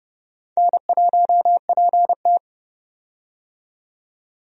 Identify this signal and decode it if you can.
Morse code, heard as a keyed tone, about 30 wpm, 700 Hz. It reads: N1PT
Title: USA Call Signs - 4 Characters - 30wpm